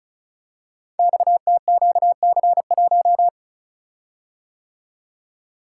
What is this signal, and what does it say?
Morse code, heard as a keyed tone, about 36 wpm, 690 Hz. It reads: XTQC1
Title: Code Groups - 5 Letter:Numbers - 35wpm